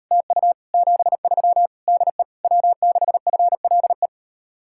Morse code, heard as a keyed tone, about 38 wpm, 700 Hz. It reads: TU 73 DE W6FLE